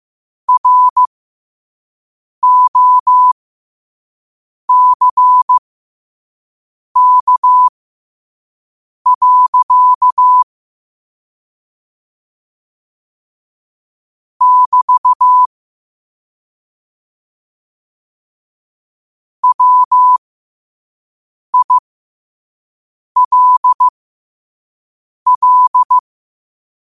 Morse code, heard as a keyed tone, about 15 wpm, spaced as for 5 wpm, 1000 Hz. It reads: ROCK. <BT> WILL
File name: qotdfiles / 5wpm